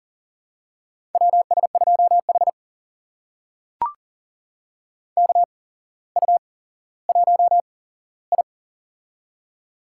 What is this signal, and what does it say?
Morse code, heard as a keyed tone, about 40 wpm, spaced as for 17 wpm, 700 Hz. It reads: WS2H KU1I
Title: USA Call Signs - 4 Characters - 8x Character Spacing - 40wpm